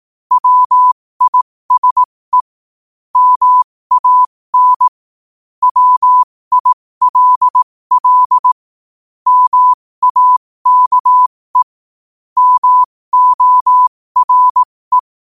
Morse code, heard as a keyed tone, about 18 wpm, spaced as for 15 wpm, 1000 Hz. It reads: WISE MAN WILL MAKE MORE
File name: qotdfiles / 15wpm